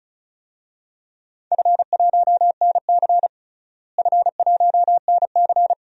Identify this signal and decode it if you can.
Morse code, heard as a keyed tone, about 36 wpm, 700 Hz. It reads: F1NC F1NC
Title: DXCC-100 Call Signs - 4 Characters - Repeated 2x - 35wpm